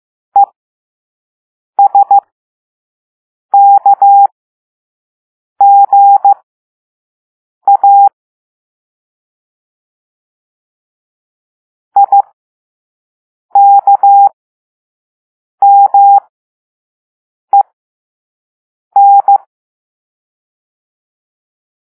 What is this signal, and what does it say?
Morse code, heard as a keyed tone, about 15 wpm, spaced as for 5 wpm, 800 Hz. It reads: ESKGA IKMEN